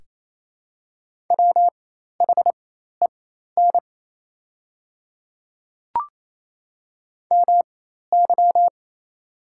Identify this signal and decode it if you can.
Morse code, heard as a keyed tone, about 28 wpm, spaced as for 13 wpm, 690 Hz. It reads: WHEN MY
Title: 4x Character Spacing - Top 200 Words - 28wpm